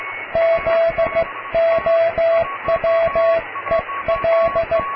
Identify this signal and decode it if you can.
Morse code, heard as a keyed tone, about 15 wpm, 660 Hz, with radio noise. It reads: ZOWEL
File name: cw / tekst(a)+QRM